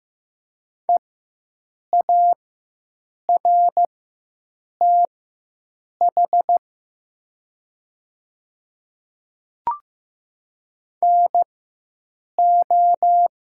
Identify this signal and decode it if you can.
Morse code, heard as a keyed tone, about 15 wpm, spaced as for 7 wpm, 700 Hz. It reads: EARTH NO
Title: 4x Character Spacing - Top 300 Words - 15wpm